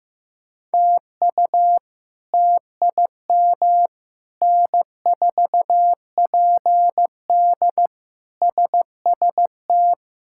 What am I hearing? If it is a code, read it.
Morse code, heard as a keyed tone, about 15 wpm, 700 Hz. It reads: TU TIM N4PD SST
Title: K1USN SST QSO Practice - 15wpm